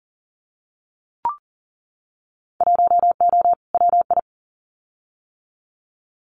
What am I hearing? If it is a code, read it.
Morse code, heard as a keyed tone, about 38 wpm, 700 Hz. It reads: 1O WI
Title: Winter Field Day Exchanges Encore 40wpm